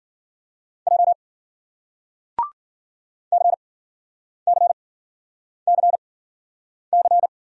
Morse code, heard as a keyed tone, about 50 wpm, spaced as for 9 wpm, 700 Hz. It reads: J CCCC
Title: Speed-Racing Single Letters 40wpm